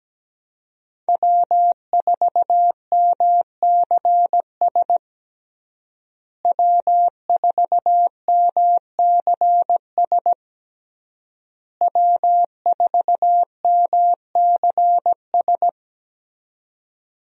Morse code, heard as a keyed tone, about 17 wpm, 700 Hz. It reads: W4MCS W4MCS W4MCS
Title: USA Call Signs - 5 Characters - Repeated 3x - 17wpm